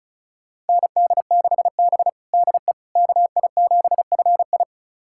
Morse code, heard as a keyed tone, about 36 wpm, 700 Hz. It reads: ND6B DE KI7FI
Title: USA Callsign DE Callsign 35wpm